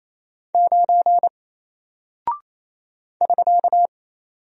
Morse code, heard as a keyed tone, about 28 wpm, 700 Hz. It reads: 9 <SK>
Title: Single Character (Letter:Number:Punctuation:Prosign) Alt 28wpm